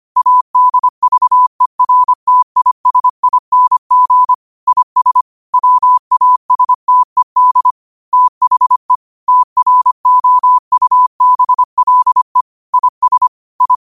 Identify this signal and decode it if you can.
Morse code, heard as a keyed tone, about 25 wpm, 1000 Hz. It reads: ADVERTISING IS WASTED THE TROUBLE IS I